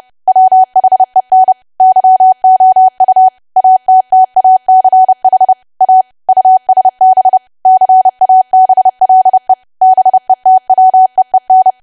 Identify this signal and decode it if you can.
Morse code, heard as a keyed tone, about 30 wpm, 750 Hz. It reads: WHEN YOU ATTACH A USB CABLE BETWEEN
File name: cwa / W1AW-Level3 / 30WPM